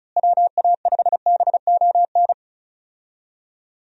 Morse code, heard as a keyed tone, about 36 wpm, 700 Hz. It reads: WA5BON
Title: USA Call Signs - 6 Characters - Repeated 2x - 35wpm